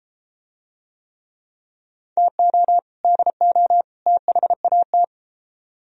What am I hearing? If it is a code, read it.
Morse code, heard as a keyed tone, about 30 wpm, 700 Hz. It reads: TO DO THAT